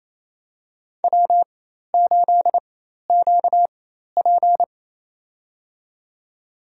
Morse code, heard as a keyed tone, about 27 wpm, spaced as for 13 wpm, 700 Hz. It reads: W8QP
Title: USA Call Signs - 4 Characters - 4x Character Spacing - 28wpm